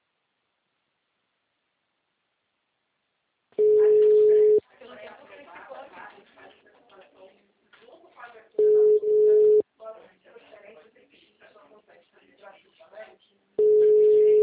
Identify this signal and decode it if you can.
Ringback tone (Europe, 425 Hz), with speech.